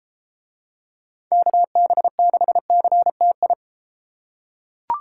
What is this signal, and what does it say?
Morse code, heard as a keyed tone, about 32 wpm, 700 Hz. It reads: KB6CTI